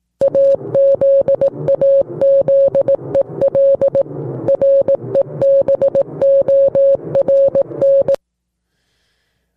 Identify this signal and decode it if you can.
Morse code, heard as a keyed tone, about 18 wpm, 550 Hz. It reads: AZAZEL REBORN